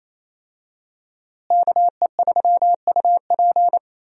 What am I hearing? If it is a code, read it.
Morse code, heard as a keyed tone, about 28 wpm, 700 Hz. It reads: KE3UP
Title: USA Call Signs - 5 Characters - Repeated 3x - 28wpm